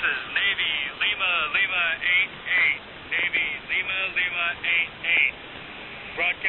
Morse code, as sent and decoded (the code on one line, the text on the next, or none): none